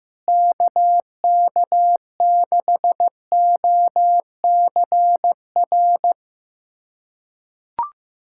Morse code, heard as a keyed tone, about 15 wpm, 700 Hz. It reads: KK6OCR
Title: USA Callsign DE Callsign 15wpm